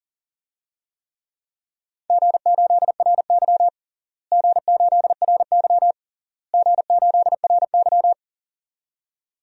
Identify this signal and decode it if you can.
Morse code, heard as a keyed tone, about 40 wpm, 700 Hz. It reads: G8RY G8RY G8RY
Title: DXCC-100 Call Signs - 4 Characters - Repeated 3x - 40wpm